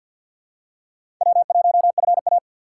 Morse code, heard as a keyed tone, about 50 wpm, 700 Hz. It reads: W1FA
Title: USA Call Signs - 4 Characters - 6x Character Spacing - 50wpm